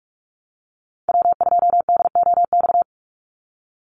Morse code, heard as a keyed tone, about 44 wpm, 700 Hz. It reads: W2DOX
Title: USA Call Signs 45wpm